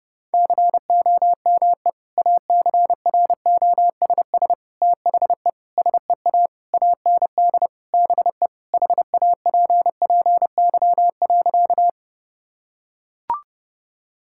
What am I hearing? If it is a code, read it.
Morse code, heard as a keyed tone, about 30 wpm, 700 Hz. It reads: COME ACROSS THE SEA AND BE HAPPY.
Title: Sentences from Top 1000 Words - Review Part 5 - 30wpm